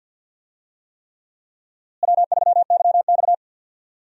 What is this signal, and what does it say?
Morse code, heard as a keyed tone, about 53 wpm, 700 Hz. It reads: W3YX
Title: USA Call Signs - 4 Characters - Repeated 3x - 50wpm